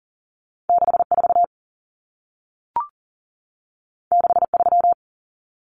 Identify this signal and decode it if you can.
Morse code, heard as a keyed tone, about 40 wpm, 700 Hz. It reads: 64 63
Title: Numbers - 2-Digit 40wpm